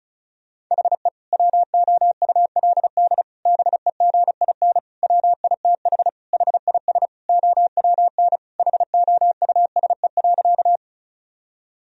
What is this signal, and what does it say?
Morse code, heard as a keyed tone, about 34 wpm, 690 Hz. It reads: HE WOULD BEGIN WITH HIS OWN HOUSE.
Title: Sentences from Top 1000 Words - Review Part 1 - 35wpm